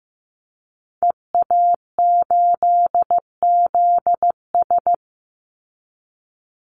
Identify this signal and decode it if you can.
Morse code, heard as a keyed tone, about 15 wpm, 700 Hz. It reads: EA8ZS
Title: Call Signs Repeated 2x - 15wpm